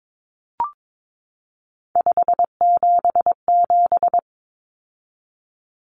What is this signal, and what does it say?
Morse code, heard as a keyed tone, about 22 wpm, 700 Hz. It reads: 577
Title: Numbers - 3-Digit 22wpm